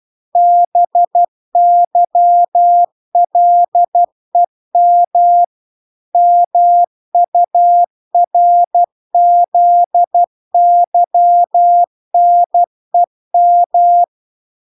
Morse code, heard as a keyed tone, about 12 wpm, 700 Hz. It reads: BYLEM MURZYNEM